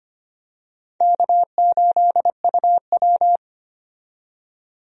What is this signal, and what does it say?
Morse code, heard as a keyed tone, about 25 wpm, 700 Hz. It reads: K8UW
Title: USA Call Signs - 4 Characters - 25wpm